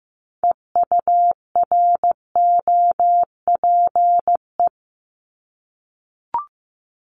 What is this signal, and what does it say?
Morse code, heard as a keyed tone, about 15 wpm, 700 Hz. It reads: EUROPE
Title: Multi-letter Phonemes 15wpm